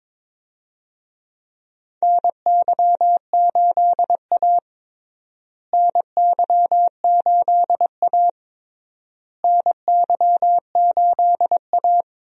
Morse code, heard as a keyed tone, about 21 wpm, 700 Hz. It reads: NY8A NY8A NY8A
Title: USA Call Signs - 4 Characters - Repeated 3x - 22wpm